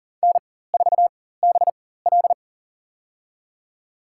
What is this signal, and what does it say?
Morse code, heard as a keyed tone, about 40 wpm, spaced as for 19 wpm, 700 Hz. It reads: N4BL
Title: USA Call Signs - 4 Characters - 4x Character Spacing - 40wpm